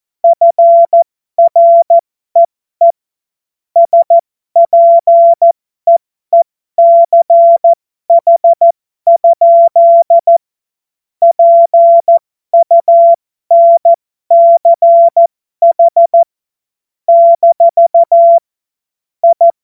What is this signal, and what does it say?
Morse code, heard as a keyed tone, about 14 wpm, spaced as for 12 wpm, 680 Hz. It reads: FREE SPEECH? PUNCH - I
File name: Morsekurs / Witze